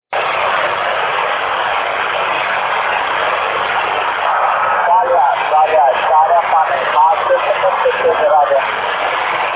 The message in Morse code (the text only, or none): none